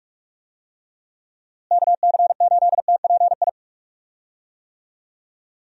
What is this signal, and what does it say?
Morse code, heard as a keyed tone, about 41 wpm, 700 Hz. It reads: KC8TPI
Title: USA Call Signs - 6 Characters - Repeated 2x - 45wpm